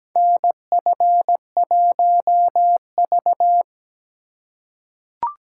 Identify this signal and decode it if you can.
Morse code, heard as a keyed tone, about 17 wpm, 700 Hz. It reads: NF1V
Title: USA Call Signs - 4 Characters - 2x Character Spacing - 17wpm